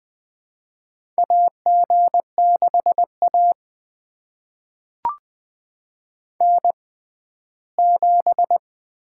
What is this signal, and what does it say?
Morse code, heard as a keyed tone, about 20 wpm, spaced as for 14 wpm, 700 Hz. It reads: AG6A N7
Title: USA Call Signs - 4 Characters - 6x Character Spacing - 20wpm